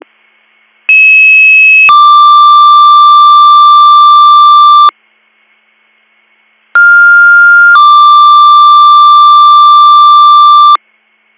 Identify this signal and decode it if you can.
Two-tone page, heard as a one-second tone, then a three-second tone, about 2600 then 1180 Hz, about 1400 then 1180 Hz. It is heard from 0.9 to 10.8 s.